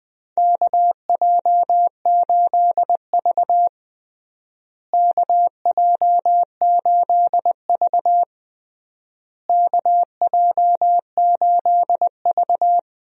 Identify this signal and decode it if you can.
Morse code, heard as a keyed tone, about 21 wpm, 700 Hz. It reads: KJ8V KJ8V KJ8V
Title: USA Call Signs - 4 Characters - Repeated 3x - 20wpm